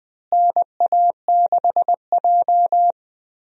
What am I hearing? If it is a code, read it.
Morse code, heard as a keyed tone, about 20 wpm, 700 Hz. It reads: NA6J
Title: USA Call Signs - 4 Characters - Repeated 3x - 20wpm